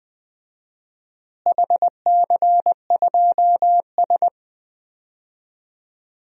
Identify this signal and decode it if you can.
Morse code, heard as a keyed tone, about 20 wpm, 700 Hz. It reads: HC2S
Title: DXCC-100 Call Signs - 4 Characters - Repeated 2x - 20wpm